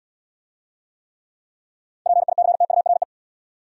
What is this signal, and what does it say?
Morse code, heard as a keyed tone, about 77 wpm, 700 Hz. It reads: PICTURE